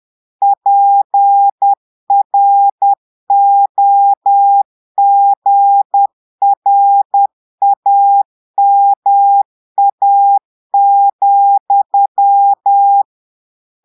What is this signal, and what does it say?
Morse code, heard as a keyed tone, about 10 wpm, 800 Hz. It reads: PROGRAMA,
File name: portugais texte en clair 2 vitesse 10 mots minutes